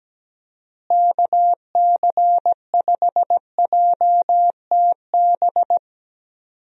Morse code, heard as a keyed tone, about 17 wpm, 700 Hz. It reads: KC5JTB
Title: USA Call Signs - 6 Characters - Repeated 3x - 17wpm